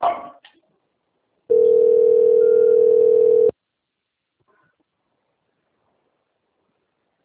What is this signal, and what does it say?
Telephone call progress tone (North America), heard as a telephone network's tone, about 440, 480 Hz. Ringback tone.